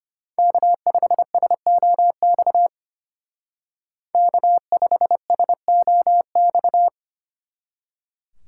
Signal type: Morse code

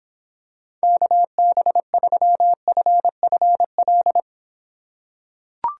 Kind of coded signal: Morse code

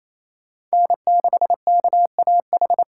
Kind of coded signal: Morse code

